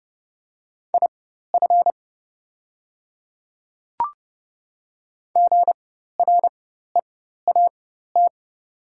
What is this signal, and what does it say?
Morse code, heard as a keyed tone, about 30 wpm, spaced as for 14 wpm, 700 Hz. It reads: IF GREAT
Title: 4x Character Spacing - Top 300 Words - 30wpm